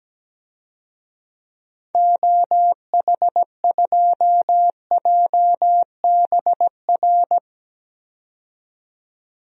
Morse code, heard as a keyed tone, about 17 wpm, 700 Hz. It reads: OH2JBR
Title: DXCC-100 Call Signs - 6 Characters - 17wpm